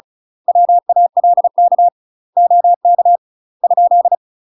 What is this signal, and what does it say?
Morse code, heard as a keyed tone, about 36 wpm, 700 Hz. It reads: WALK OK ?
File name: UK MASTER QSOs TRAINING / 35WPM / UK MASTER QSOs 0007